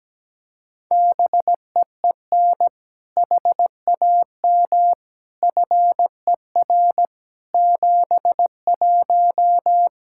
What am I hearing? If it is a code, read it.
Morse code, heard as a keyed tone, about 17 wpm, 700 Hz. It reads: BEEN HAM FER 71
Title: Practice QSO - Exchange 3 - More Info - 17wpm